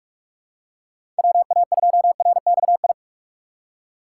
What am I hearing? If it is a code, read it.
Morse code, heard as a keyed tone, about 44 wpm, 700 Hz. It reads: WA2RXI
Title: USA Call Signs - 6 Characters - 2x Character Spacing - 45wpm